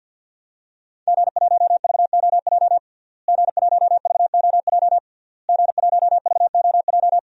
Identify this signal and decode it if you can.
Morse code, heard as a keyed tone, about 50 wpm, 700 Hz. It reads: G1VOJ G1VOJ G1VOJ
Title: DXCC-100 Call Signs - 5 Characters - Repeated 3x - 50wpm